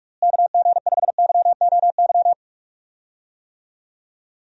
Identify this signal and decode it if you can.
Morse code, heard as a keyed tone, about 46 wpm, 700 Hz. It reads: KG5YOY
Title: USA Call Signs - 6 Characters - Repeated 3x - 45wpm